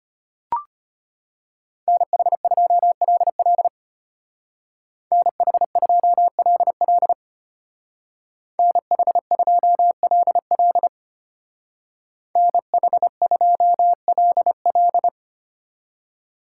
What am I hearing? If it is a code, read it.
Morse code, heard as a keyed tone, about 30 wpm, 700 Hz. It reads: NH2LL NH2LL NH2LL NH2LL
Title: Speed Racing - USA 2x2 Call Signs 25wpm